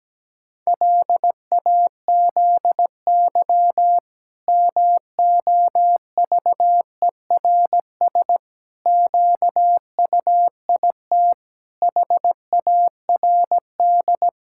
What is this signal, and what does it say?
Morse code, heard as a keyed tone, about 17 wpm, 700 Hz. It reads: LAZY MOVERS QUIT HARD
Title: Pangram Challenge v2 - 17wpm